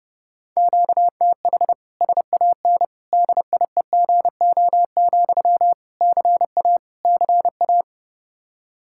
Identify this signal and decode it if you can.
Morse code, heard as a keyed tone, about 30 wpm, 700 Hz. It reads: QTH SAN DIEGO, CA CA
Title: Practice QSO - Exchange 1 - Beginning - 30wpm